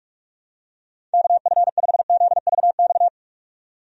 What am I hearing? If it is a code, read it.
Morse code, heard as a keyed tone, about 41 wpm, 700 Hz. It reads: KF5ZVX
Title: USA Call Signs - 6 Characters - 45wpm